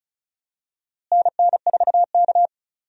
Morse code, heard as a keyed tone, about 36 wpm, 700 Hz. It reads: NN4K
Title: USA Call Signs - 4 Characters - Repeated 3x - 35wpm